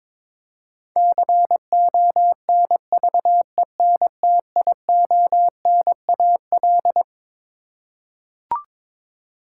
Morse code, heard as a keyed tone, about 22 wpm, 700 Hz. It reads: CONVENTIONAL
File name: Twelve Letter Words - 22wpm